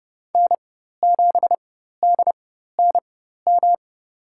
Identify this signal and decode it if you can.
Morse code, heard as a keyed tone, about 29 wpm, spaced as for 14 wpm, 700 Hz. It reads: N7DNM
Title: USA Call Signs - 5 Characters - 4x Character Spacing - 30wpm